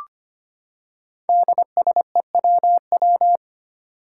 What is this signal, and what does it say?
Morse code, heard as a keyed tone, about 25 wpm, 700 Hz. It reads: DSEWW
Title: Five Challenging Letter-Number Combinations 25wpm